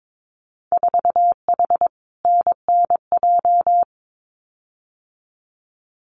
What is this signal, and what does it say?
Morse code, heard as a keyed tone, about 22 wpm, 700 Hz. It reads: 4H NNJ
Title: Winter Field Day Exchanges Encore 22wpm